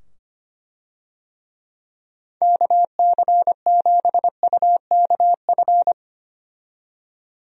Morse code, heard as a keyed tone, about 25 wpm, 700 Hz. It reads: KC7UKF
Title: USA Call Signs - 6 Characters - 25wpm